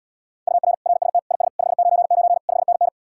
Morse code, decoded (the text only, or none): UR RST IS 599 5NN